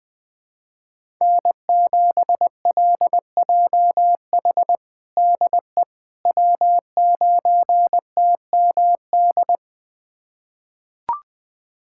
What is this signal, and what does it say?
Morse code, heard as a keyed tone, about 20 wpm, 700 Hz. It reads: N7LJH DE W9TMD
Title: USA Callsign DE Callsign - 5 Characters - 20wpm